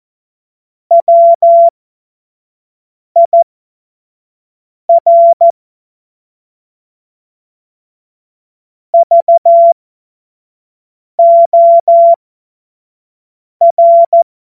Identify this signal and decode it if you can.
Morse code, heard as a keyed tone, about 14 wpm, spaced as for 5 wpm, 680 Hz. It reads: WIR VOR
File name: Morsekurs / Witze / Witz 11. Januar 2022, 25 F70